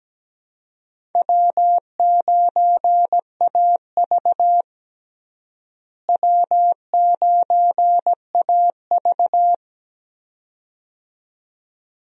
Morse code, heard as a keyed tone, about 17 wpm, 700 Hz. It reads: W9AV W9AV
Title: USA Call Signs - 4 Characters - Repeated 2x - 17wpm